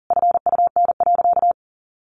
Morse code, heard as a keyed tone, about 40 wpm, 700 Hz. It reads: FUN.
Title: Easy Phrases 40wpm